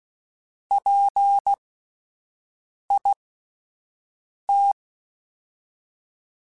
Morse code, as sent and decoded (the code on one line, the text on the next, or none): .--. .. -
PIT